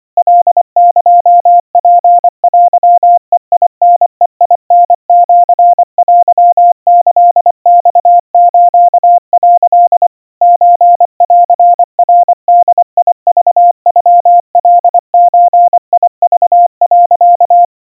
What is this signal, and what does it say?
Morse code (Japanese Wabun code), heard as a keyed tone, about 25 wpm, 700 Hz. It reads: カエツテベタベタシテキマス」 ソンナボクノカゾク、
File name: 90 cpm 700 Hz Crispy